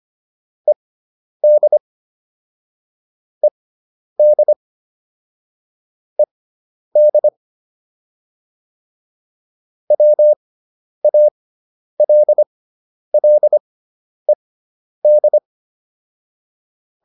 Morse code, heard as a keyed tone, about 25 wpm, spaced as for 10 wpm, 600 Hz. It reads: ED ED ED WALLED